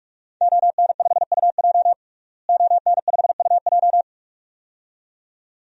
Morse code, heard as a keyed tone, about 46 wpm, 700 Hz. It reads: ON5UJ ON5UJ
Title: DXCC-100 Call Signs - 5 Characters - Repeated 2x - 45wpm